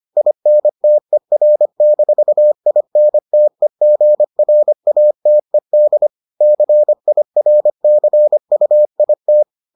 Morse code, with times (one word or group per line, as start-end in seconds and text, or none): 0.2-6.1 INTER-INTEGRATED
6.4-9.4 CIRCUIT